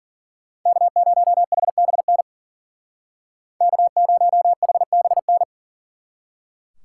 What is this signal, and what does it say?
Morse code, heard as a keyed tone, about 50 wpm, 700 Hz. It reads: K0HBN K0HBN